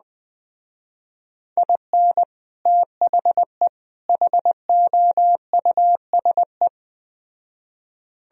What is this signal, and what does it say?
Morse code, heard as a keyed tone, about 20 wpm, 700 Hz. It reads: IN THE HOUSE